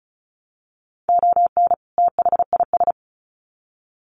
Morse code, heard as a keyed tone, about 34 wpm, 700 Hz. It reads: ON THIS
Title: Sets of 2 Words 35wpm